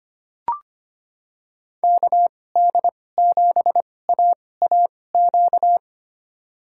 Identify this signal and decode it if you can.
Morse code, heard as a keyed tone, about 25 wpm, spaced as for 18 wpm, 700 Hz. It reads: KD7AAQ
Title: USA Call Signs - 2x Character Spacing - 25wpm